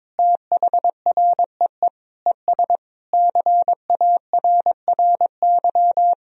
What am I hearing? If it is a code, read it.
Morse code, heard as a keyed tone, about 22 wpm, 700 Hz. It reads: THREE ES CARRY
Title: Sentences from Top 400 Words and Abbreviations 22wpm